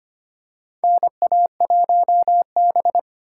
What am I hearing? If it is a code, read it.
Morse code, heard as a keyed tone, about 25 wpm, 700 Hz. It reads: NA1B
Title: USA Call Signs - 4 Characters - 2x Character Spacing - 25wpm